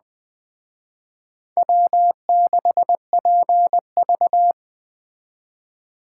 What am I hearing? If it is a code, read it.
Morse code, heard as a keyed tone, about 21 wpm, 700 Hz. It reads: W6PV